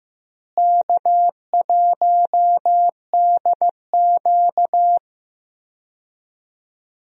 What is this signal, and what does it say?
Morse code, heard as a keyed tone, about 15 wpm, 700 Hz. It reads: K1DQ
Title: USA Call Signs - 4 Characters - Repeated 3x - 15wpm